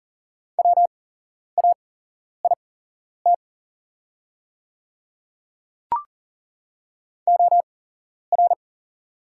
Morse code, heard as a keyed tone, about 40 wpm, spaced as for 11 wpm, 700 Hz. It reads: WAIT OR
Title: 8x Character Spacing - Top 500 Words - 40wpm